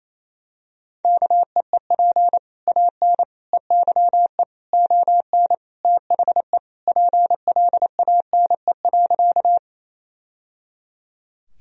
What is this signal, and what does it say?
Morse code, heard as a keyed tone, about 28 wpm, 700 Hz. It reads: KEEP AN EYE ON THE PLANE.
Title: Sentences from Top 1000 Words - Review Part 3 - 28wpm